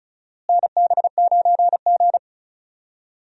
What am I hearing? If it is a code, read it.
Morse code, heard as a keyed tone, about 35 wpm, 700 Hz. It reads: NB9G
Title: USA Call Signs - 4 Characters - Repeated 3x - 35wpm